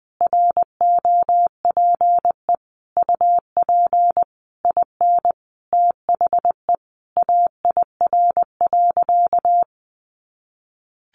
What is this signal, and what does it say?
Morse code, heard as a keyed tone, about 20 wpm, 700 Hz. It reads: ROPE UP IN THE AIR.
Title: Easy Phrases 20wpm